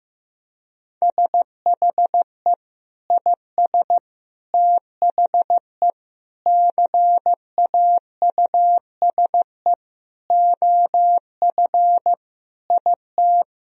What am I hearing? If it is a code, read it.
Morse code, heard as a keyed tone, about 15 wpm, 700 Hz. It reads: SHE IS THE CAUSE OF IT
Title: Sentences from Top 1000 Words - Review Part 1 - 15wpm